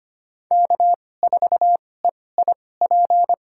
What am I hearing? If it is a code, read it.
Morse code, heard as a keyed tone, about 25 wpm, spaced as for 18 wpm, 700 Hz. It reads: K4EIP
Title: USA Call Signs - 2x Character Spacing - 25wpm